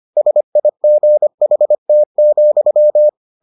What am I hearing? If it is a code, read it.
Morse code, heard as a keyed tone, about 25 wpm, 600 Hz. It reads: SIGHT,